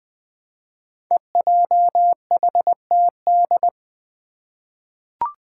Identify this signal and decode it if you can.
Morse code, heard as a keyed tone, about 20 wpm, 700 Hz. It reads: EJHTD
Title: Five Challenging Letter-Number Combinations 20wpm